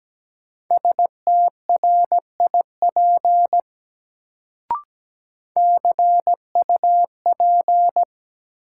Morse code, heard as a keyed tone, about 17 wpm, 700 Hz. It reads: STRIP CUP
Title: Sentences from Top 1600 Words - New Word Drill - 17wpm